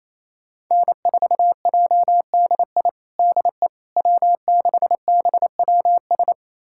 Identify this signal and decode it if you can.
Morse code, heard as a keyed tone, about 28 wpm, 700 Hz. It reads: N4JDI DE W6BWS
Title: USA Callsign DE Callsign - 5 Characters - 28wpm